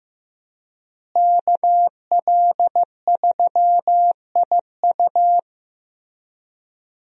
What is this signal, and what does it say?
Morse code, heard as a keyed tone, about 15 wpm, 700 Hz. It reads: KL3IU
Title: USA Call Signs - 5 Characters - Repeated 3x - 15wpm